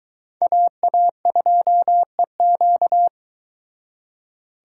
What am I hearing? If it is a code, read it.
Morse code, heard as a keyed tone, about 22 wpm, 700 Hz. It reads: AA2EQ